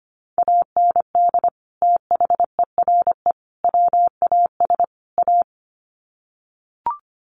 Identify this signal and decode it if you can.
Morse code, heard as a keyed tone, about 25 wpm, 700 Hz. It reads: AND THERE WAS A
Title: Sets of 4 Words 25wpm